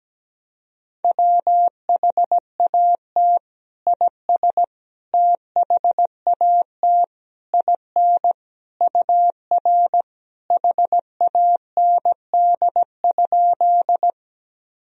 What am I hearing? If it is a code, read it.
Morse code, heard as a keyed tone, about 17 wpm, 700 Hz. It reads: WHAT IS THAT IN UR HAND?
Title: Sentences from Top 200 Words and Abbreviations 17wpm